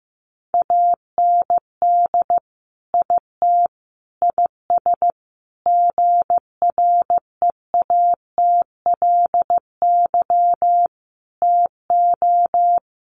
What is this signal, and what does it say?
Morse code, heard as a keyed tone, about 15 wpm, 700 Hz. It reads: AND IT IS GREATLY TO
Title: Sentences from Top 1400 Words 15wpm